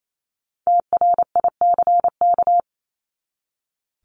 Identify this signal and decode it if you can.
Morse code, heard as a keyed tone, about 28 wpm, 700 Hz. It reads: TRICK
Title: Top 1251-1500 Words 28wpm